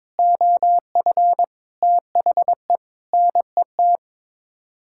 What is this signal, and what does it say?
Morse code, heard as a keyed tone, about 22 wpm, 700 Hz. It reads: OF THE NET
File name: practice / 22wpm / Sets of 8 Words - 22wpm